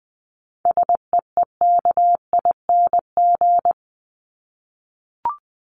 Morse code, heard as a keyed tone, about 20 wpm, 700 Hz. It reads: SEEKING
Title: ING Suffix - Top 100 Words with Base Words 20wpm